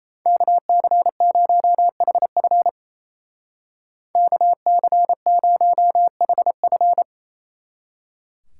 Morse code, heard as a keyed tone, about 33 wpm, 700 Hz. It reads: KC0HF KC0HF
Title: Speed Racing - USA 2x2 Call Signs 28wpm